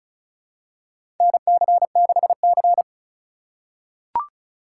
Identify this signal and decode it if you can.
Morse code, heard as a keyed tone, about 36 wpm, 700 Hz. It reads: NC6C
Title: USA Call Signs - 4 Characters - 4x Character Spacing - 35wpm